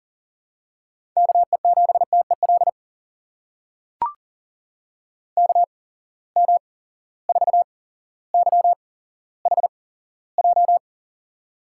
Morse code, heard as a keyed tone, about 38 wpm, spaced as for 17 wpm, 700 Hz. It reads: KE7TEL KM4YHJ
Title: USA Call Signs - 6 Characters - 8x Character Spacing - 40wpm